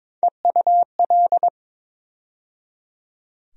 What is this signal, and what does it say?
Morse code, heard as a keyed tone, about 22 wpm, 700 Hz. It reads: EUL
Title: USA Call Sign Postfixes - 3 Letters 22wpm